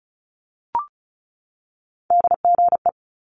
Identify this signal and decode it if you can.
Morse code, heard as a keyed tone, about 35 wpm, 700 Hz. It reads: DGE
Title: Multi-letter Phonemes 35wpm